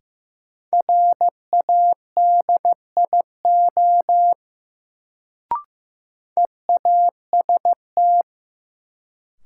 Morse code, heard as a keyed tone, about 15 wpm, 700 Hz. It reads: RADIO EAST
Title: Sentences from Top 1000 Words - Review Part 3 - New Word Drill - 15wpm